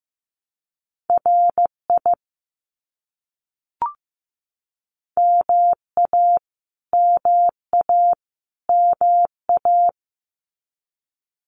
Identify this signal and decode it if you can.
Morse code, heard as a keyed tone, about 15 wpm, 700 Hz. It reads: RI MA MA MA
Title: US States Repeated 3x 15wpm